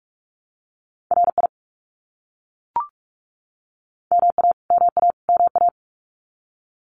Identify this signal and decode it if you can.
Morse code, heard as a keyed tone, about 44 wpm, 700 Hz. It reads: RI MA MA MA
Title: US States Repeated 3x 45wpm